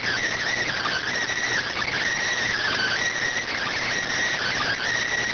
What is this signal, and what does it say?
Emergency Alert System (SAME) tones (header bursts), heard as harsh buzzing bursts. It is heard from 0.6 to 4.8 s.